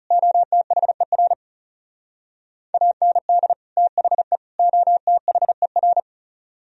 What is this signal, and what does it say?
Morse code, heard as a keyed tone, about 38 wpm, 700 Hz. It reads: OTHER AND THE OTHER